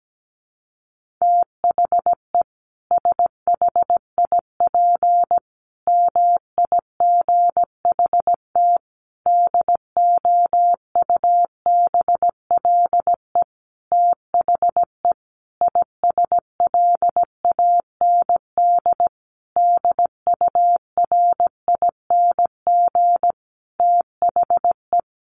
Morse code, heard as a keyed tone, about 17 wpm, 700 Hz. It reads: THE SHIP MIGHT DOUBLE THE ISLAND DURING THE
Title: Sentences from Top 1400 Words 17wpm